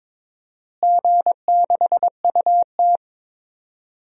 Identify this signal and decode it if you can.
Morse code, heard as a keyed tone, about 21 wpm, 700 Hz. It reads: G6UT
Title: DXCC-100 Call Signs - 4 Characters - Repeated 2x - 22wpm